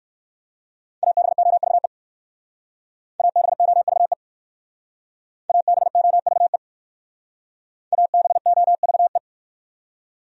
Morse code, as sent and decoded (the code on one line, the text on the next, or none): .- -... --- ...- . / .- -... --- ...- . / .- -... --- ...- . / .- -... --- ...- .
ABOVE ABOVE ABOVE ABOVE